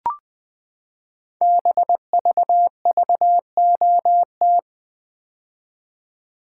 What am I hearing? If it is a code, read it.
Morse code, heard as a keyed tone, about 20 wpm, 700 Hz. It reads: BVVOT
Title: Code Groups - 5 Letter:Numbers - 20wpm